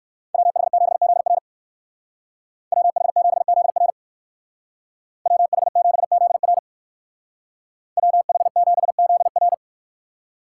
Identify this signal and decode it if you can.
Morse code, heard as a keyed tone, about 56 wpm, 700 Hz. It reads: WH7ZR WH7ZR WH7ZR WH7ZR